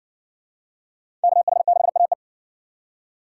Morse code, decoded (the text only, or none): KH6AE